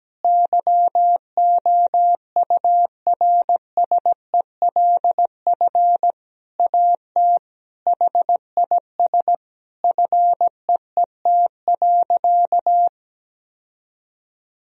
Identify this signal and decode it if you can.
Morse code, heard as a keyed tone, about 17 wpm, 700 Hz. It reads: YOURSELF AT HIS FEET.
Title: Sentences from Top 1000 Words - Review Part 1 - 17wpm